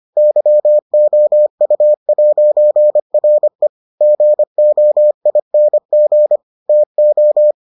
Morse code, heard as a keyed tone, about 25 wpm, 600 Hz. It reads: YOU'RE GOING TO